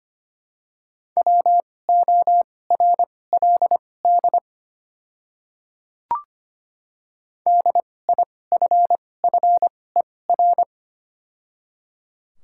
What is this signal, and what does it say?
Morse code, heard as a keyed tone, about 25 wpm, spaced as for 18 wpm, 700 Hz. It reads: WORLD DIFFER
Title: 2x Character Spacing - Top 200 Words - 25wpm